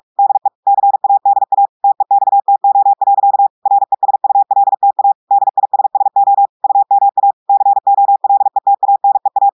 Morse code, heard as a keyed tone, about 46 wpm, 800 Hz. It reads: DE CADA TEXTO. RESULTA DISSO UMA COLETANEA
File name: portugais texte en clair 1 vitesse 45 mots minutes